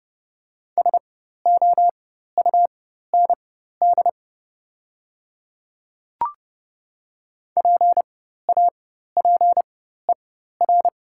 Morse code, heard as a keyed tone, about 30 wpm, spaced as for 14 wpm, 700 Hz. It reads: SOUND PAPER
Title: 4x Character Spacing - Top 300 Words - 30wpm